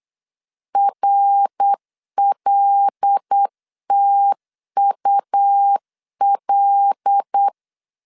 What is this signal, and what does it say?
Morse code, heard as a keyed tone, about 8 wpm, 800 Hz. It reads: RLTUL
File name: 8wpm